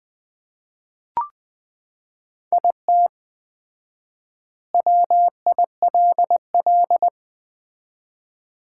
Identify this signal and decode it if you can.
Morse code, heard as a keyed tone, about 20 wpm, 700 Hz. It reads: IT WILL
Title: 4x Word-Spacing - Sets of 2 Words - 20wpm